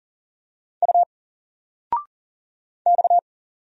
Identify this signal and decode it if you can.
Morse code, heard as a keyed tone, about 39 wpm, 700 Hz. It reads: U X
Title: Single Letter-Number 40wpm